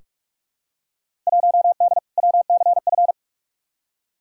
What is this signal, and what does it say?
Morse code, heard as a keyed tone, about 46 wpm, 700 Hz. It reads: 1D WCF
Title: Field Day Exchanges - 45wpm